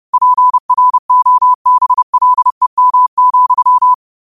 Morse code, heard as a keyed tone, about 30 wpm, 1000 Hz. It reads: PROBLEM,
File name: 30wpm